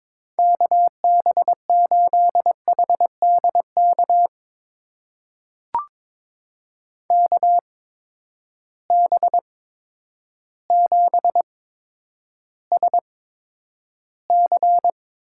Morse code, heard as a keyed tone, about 22 wpm, spaced as for 10 wpm, 700 Hz. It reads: KB8HDK KB7SC
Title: USA Call Signs - 8x Character Spacing - 22wpm